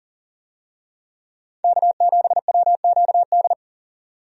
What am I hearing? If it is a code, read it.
Morse code, heard as a keyed tone, about 40 wpm, 700 Hz. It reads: K7WQD
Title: USA Call Signs - 5 Characters - 40wpm